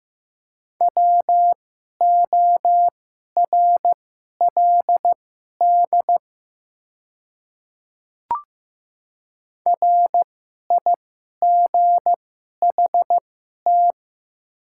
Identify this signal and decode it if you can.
Morse code, heard as a keyed tone, about 15 wpm, spaced as for 11 wpm, 700 Hz. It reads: WORLD RIGHT
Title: 2x Character Spacing - Top 200 Words - 15wpm